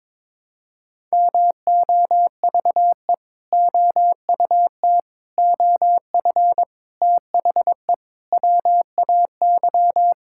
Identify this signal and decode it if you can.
Morse code, heard as a keyed tone, about 22 wpm, 700 Hz. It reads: MOVE OUT OF THE WAY
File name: practice / 22wpm / Sentences with 4-Word Phrases - 22wpm